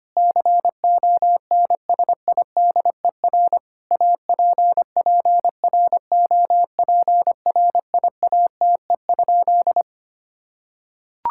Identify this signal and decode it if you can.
Morse code, heard as a keyed tone, about 25 wpm, 700 Hz. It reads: CONSIDER APPROPRIATE?
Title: Sentences from Top 1000 Words - Review Part 5 - 25wpm